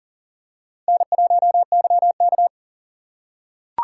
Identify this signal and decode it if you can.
Morse code, heard as a keyed tone, about 40 wpm, 700 Hz. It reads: N1YK
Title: USA Call Signs - 4 Characters - 2x Character Spacing - 40wpm